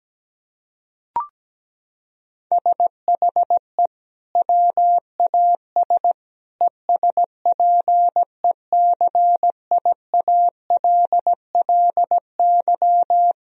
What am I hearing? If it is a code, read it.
Morse code, heard as a keyed tone, about 17 wpm, 700 Hz. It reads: SHE WAS ESPECIALLY
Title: Sentences from Top 1000 Words - Review Part 4 - 17wpm